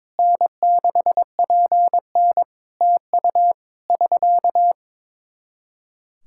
Morse code, heard as a keyed tone, about 21 wpm, 700 Hz. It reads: N6PN TU <SK>
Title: K1USN SST QSO Practice 22wpm